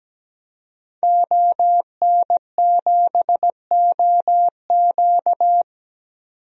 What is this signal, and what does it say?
Morse code, heard as a keyed tone, about 17 wpm, 700 Hz. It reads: ON7OQ
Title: DXCC-100 Call Signs - 5 Characters - 17wpm